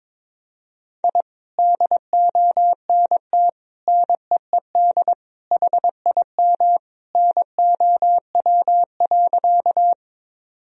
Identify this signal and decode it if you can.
Morse code, heard as a keyed tone, about 22 wpm, 700 Hz. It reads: I DONT NEED HIM NOW.
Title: Sentences from Top 1000 Words - Review Part 1 - 22wpm